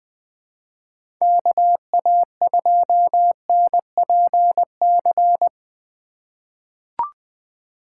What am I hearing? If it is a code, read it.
Morse code, heard as a keyed tone, about 21 wpm, 700 Hz. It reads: KA2NPC
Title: USA Call Signs - 4x Character Spacing - 20wpm